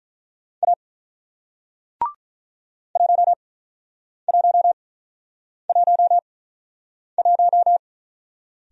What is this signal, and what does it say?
Morse code, heard as a keyed tone, about 45 wpm, spaced as for 9 wpm, 700 Hz. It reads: A 1111